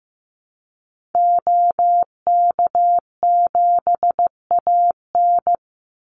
Morse code, heard as a keyed tone, about 15 wpm, 700 Hz. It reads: OK7AN
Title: Call Signs Repeated 3x - 15wpm